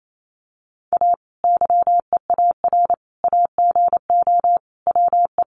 Morse code, heard as a keyed tone, about 28 wpm, 700 Hz. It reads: A YEAR AGO WE
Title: Sentences from Top 1400 Words 28wpm